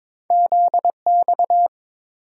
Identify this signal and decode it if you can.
Morse code, heard as a keyed tone, about 22 wpm, 700 Hz. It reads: ZX
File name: practice / 22wpm / Two Letter-Number Combinations - 22wpm